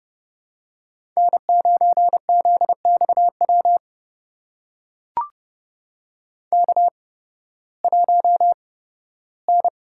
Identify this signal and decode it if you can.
Morse code, heard as a keyed tone, about 30 wpm, spaced as for 16 wpm, 700 Hz. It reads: N9ZXW K1N
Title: USA Call Signs - 5 Characters - 8x Character Spacing - 30wpm